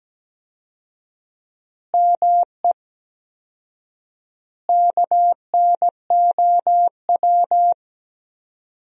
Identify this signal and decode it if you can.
Morse code, heard as a keyed tone, about 17 wpm, 700 Hz. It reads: ME KNOW
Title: 4x Word-Spacing - Sentences from Top 300 Words - 17wpm